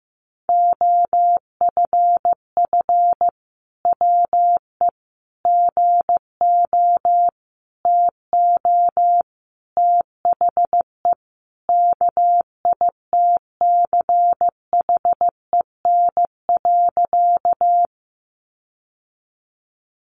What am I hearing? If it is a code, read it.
Morse code, heard as a keyed tone, about 15 wpm, 700 Hz. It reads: OFF WE GO TO THE KITCHEN.
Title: Sentences from Top 1400 Words 15wpm